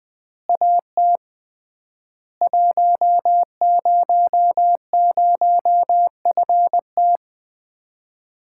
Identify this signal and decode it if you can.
Morse code, heard as a keyed tone, about 20 wpm, 700 Hz. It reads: AT 100FT
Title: Common QSO Phrases - 3x Word-Spacing - 20wpm